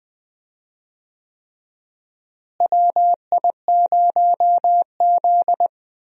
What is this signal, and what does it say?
Morse code, heard as a keyed tone, about 20 wpm, 700 Hz. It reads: WI0Z
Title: USA Call Signs - 4 Characters - Repeated 3x - 20wpm